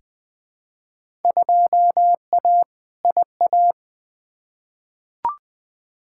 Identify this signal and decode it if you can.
Morse code, heard as a keyed tone, about 21 wpm, 700 Hz. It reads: 2A IA
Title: Field Day Exchanges Encore - 20wpm